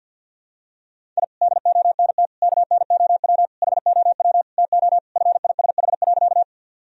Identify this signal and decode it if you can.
Morse code, heard as a keyed tone, about 51 wpm, 700 Hz. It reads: I DONT KNOW HOW TO FISH.
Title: Sentences from Top 1000 Words - Review Part 1 - 50wpm